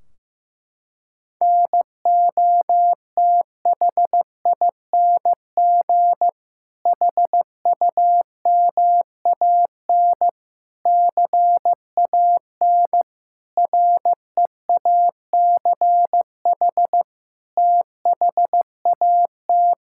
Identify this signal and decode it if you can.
Morse code, heard as a keyed tone, about 15 wpm, 700 Hz. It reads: NOTHING HUMAN CAN REACH THAT
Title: Sentences from Top 1600 Words 15wpm